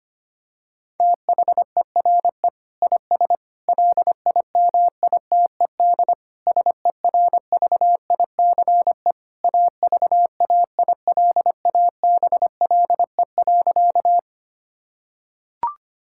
Morse code, heard as a keyed tone, about 25 wpm, 700 Hz. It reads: THERE IS LIMITED SERVICE AVAILABLE.
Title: Sentences from Top 1000 Words - Review Part 6 - 25wpm